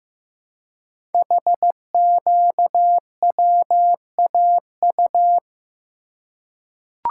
Morse code, heard as a keyed tone, about 15 wpm, 700 Hz. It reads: HQWAU
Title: Code Groups - 5 Letters - 15wpm